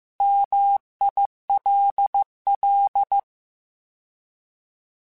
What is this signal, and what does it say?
Morse code, heard as a keyed tone, about 15 wpm, 800 Hz. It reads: MILL
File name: Alphabetical / Alphabetical mp3s / Ms @13